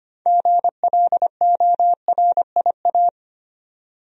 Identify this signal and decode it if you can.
Morse code, heard as a keyed tone, about 25 wpm, 700 Hz. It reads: GLORIA
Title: USA Common Names - 25wpm